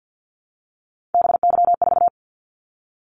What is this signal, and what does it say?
Morse code, heard as a keyed tone, about 50 wpm, 700 Hz. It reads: BY4
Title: Callsign Prefixes for Top 100 DXCC - 50wpm